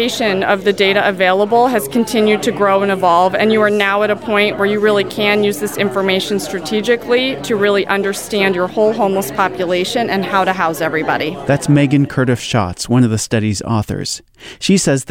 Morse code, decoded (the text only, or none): none